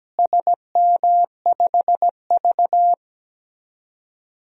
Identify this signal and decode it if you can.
Morse code, heard as a keyed tone, about 17 wpm, 700 Hz. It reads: SM5V
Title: DXCC-100 Call Signs - 4 Characters - Repeated 2x - 17wpm